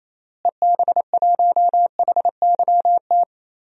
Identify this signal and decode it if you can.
Morse code, heard as a keyed tone, about 28 wpm, 700 Hz. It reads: EB1HYT
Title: DXCC-100 Call Signs - 6 Characters - Repeated 3x - 28wpm